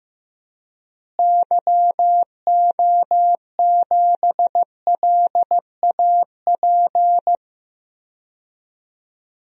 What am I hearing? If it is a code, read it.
Morse code, heard as a keyed tone, about 15 wpm, 700 Hz. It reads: YO7LAP
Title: DXCC-100 Call Signs - 6 Characters - Repeated 3x - 15wpm